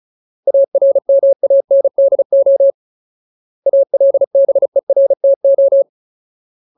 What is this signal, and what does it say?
Morse code, heard as a keyed tone, about 34 wpm, 550 Hz. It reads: ARMANDO ALBERTO
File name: Ultimate Word List / 35WPM Ultimate